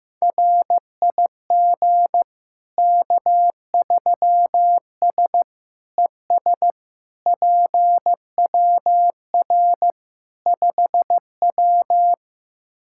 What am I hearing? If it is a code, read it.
Morse code, heard as a keyed tone, about 15 wpm, 700 Hz. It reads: RIG K3S ES PWR 5W
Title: K1USN SST QSO Practice 15wpm